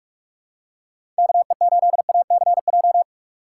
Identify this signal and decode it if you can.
Morse code, heard as a keyed tone, about 46 wpm, 700 Hz. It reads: KE8ACJ
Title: USA Call Signs - 6 Characters - 6x Character Spacing - 45wpm